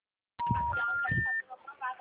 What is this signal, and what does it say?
Special information tone, with speech.